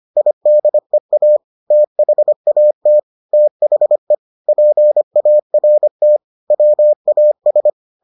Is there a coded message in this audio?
Yes: Morse code from 0.2 to 7.7 s, heard as a keyed tone, about 25 wpm, 600 Hz. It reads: IDEA THAT THE PART WAS